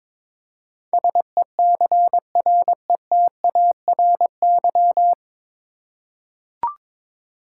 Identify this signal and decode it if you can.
Morse code, heard as a keyed tone, about 22 wpm, 700 Hz. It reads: SECRETARY
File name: 9 Letter Words - First 3 Letters Given - 22wpm